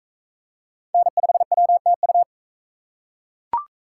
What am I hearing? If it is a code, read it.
Morse code, heard as a keyed tone, about 42 wpm, 700 Hz. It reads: N5WTU